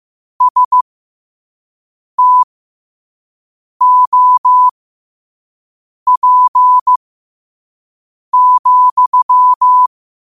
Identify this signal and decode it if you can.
Morse code, heard as a keyed tone, about 15 wpm, spaced as for 5 wpm, 1000 Hz. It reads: STOP,
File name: qotdfiles / 5wpm